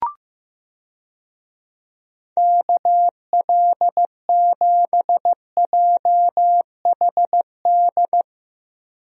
Morse code, heard as a keyed tone, about 14 wpm, 700 Hz. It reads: KL7JHD
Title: USA Call Signs - 6 Characters - Repeated 3x - 15wpm